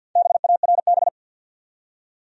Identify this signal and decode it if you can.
Morse code, heard as a keyed tone, about 50 wpm, 700 Hz. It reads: BARB
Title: USA Common Names - Max 4 Letters - 50wpm